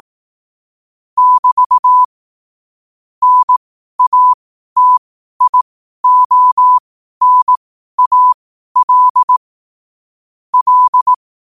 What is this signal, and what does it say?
Morse code, heard as a keyed tone, about 18 wpm, spaced as for 12 wpm, 1000 Hz. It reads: <BT> NATIONAL L
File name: qotdfiles / 12wpm